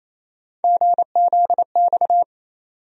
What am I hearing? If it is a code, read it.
Morse code, heard as a keyed tone, about 28 wpm, 700 Hz. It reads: GZX
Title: USA Call Sign Postfixes - 3 Letters 28wpm